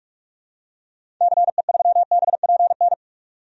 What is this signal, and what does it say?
Morse code, heard as a keyed tone, about 45 wpm, 700 Hz. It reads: CE3BPN
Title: DXCC-100 Call Signs - 6 Characters - 45wpm